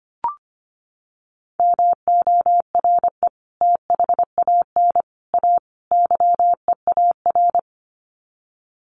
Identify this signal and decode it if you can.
Morse code, heard as a keyed tone, about 25 wpm, 700 Hz. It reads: MORE THAN A YEAR
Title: Sets of 4 Words 25wpm